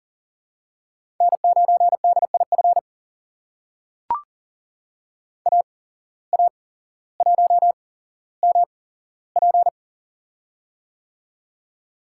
Morse code, heard as a keyed tone, about 39 wpm, spaced as for 17 wpm, 700 Hz. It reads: N9DIF AA1MP
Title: USA Call Signs - 5 Characters - 8x Character Spacing - 40wpm